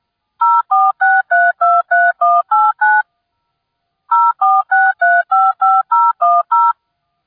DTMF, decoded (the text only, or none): 046323189 046355010